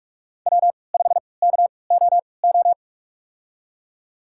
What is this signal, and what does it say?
Morse code, heard as a keyed tone, about 44 wpm, spaced as for 25 wpm, 700 Hz. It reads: W5KOO
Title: USA Call Signs - 5 Characters - 3x Character Spacing - 45wpm